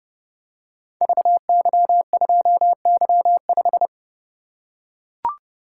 Morse code, heard as a keyed tone, about 30 wpm, 700 Hz. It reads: VY2Y5
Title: Code Groups - 5 Letter:Numbers - 30wpm